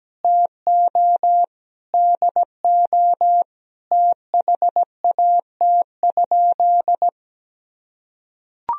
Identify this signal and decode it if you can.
Morse code, heard as a keyed tone, about 17 wpm, 700 Hz. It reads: TO DO THAT?
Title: Sentences from Top 1000 Words - Review Part 2 - 17wpm